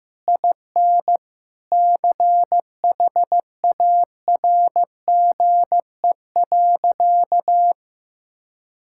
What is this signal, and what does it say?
Morse code, heard as a keyed tone, about 15 wpm, 700 Hz. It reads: IN CHARGE.
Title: Sentences from Top 1000 Words - Review Part 6 - 15wpm